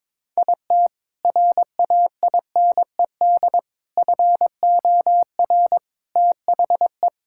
Morse code, heard as a keyed tone, about 22 wpm, 700 Hz. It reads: IT RAINED FOR THE
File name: Sentences with 7-Word Phrases - 22wpm